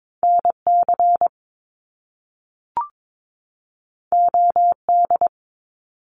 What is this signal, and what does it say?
Morse code, heard as a keyed tone, about 22 wpm, 700 Hz. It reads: NC OD
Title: Top Two Letter Combinations in Words 22wpm